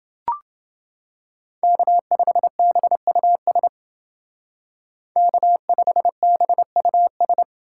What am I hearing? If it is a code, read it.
Morse code, heard as a keyed tone, about 29 wpm, 700 Hz. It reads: K5BUS K5BUS